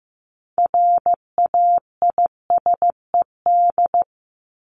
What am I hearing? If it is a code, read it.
Morse code, heard as a keyed tone, about 15 wpm, 700 Hz. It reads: RAISED
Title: ED Suffix - Top 100 Words with Base Words 15wpm